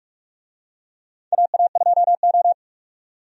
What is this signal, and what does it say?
Morse code, heard as a keyed tone, about 46 wpm, 700 Hz. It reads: AA2O
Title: USA Call Signs - 4 Characters - 45wpm